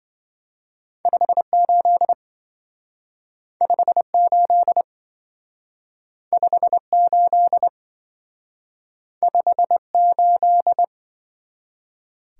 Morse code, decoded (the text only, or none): 58 58 58 58